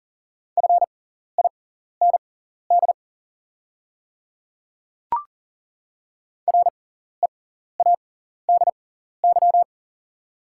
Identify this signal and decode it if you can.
Morse code, heard as a keyed tone, about 40 wpm, spaced as for 14 wpm, 690 Hz. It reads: FIND READY
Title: 6x Character Spacing - Top 500 Words - 40wpm